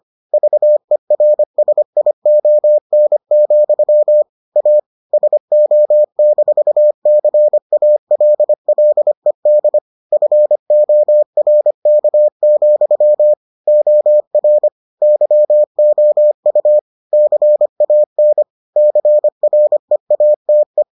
Morse code, heard as a keyed tone, about 25 wpm, 600 Hz. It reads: VERSION, A SO-CALLED FORK, OR YOU CAN CREATE